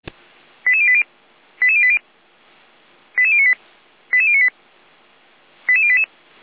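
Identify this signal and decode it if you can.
Five-tone selective call (ZVEI), heard as quick runs of five tones, tones of about 70 ms. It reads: 88080 88080 88098 88098 88088
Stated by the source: ZVEI five-tone selective call 88088 received by SDR.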